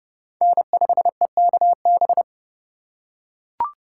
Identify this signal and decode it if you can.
Morse code, heard as a keyed tone, about 30 wpm, 700 Hz. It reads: N5EKB